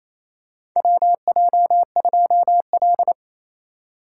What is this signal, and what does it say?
Morse code, heard as a keyed tone, about 27 wpm, 700 Hz. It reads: WJ2L
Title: K1USN SST QSO Practice - 28wpm